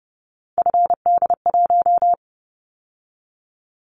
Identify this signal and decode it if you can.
Morse code, heard as a keyed tone, about 30 wpm, 700 Hz. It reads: FD1
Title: Callsign Prefixes for Top 25 DXCC - 30wpm